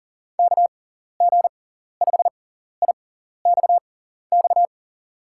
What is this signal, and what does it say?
Morse code, heard as a keyed tone, about 37 wpm, spaced as for 14 wpm, 700 Hz. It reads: KG5IXX
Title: USA Call Signs - 6x Character Spacing - 40wpm